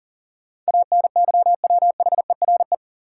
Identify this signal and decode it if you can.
Morse code, heard as a keyed tone, about 40 wpm, 700 Hz. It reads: ANYWHERE
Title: Sentences from Top 1400 Words - New Word Drill - 40wpm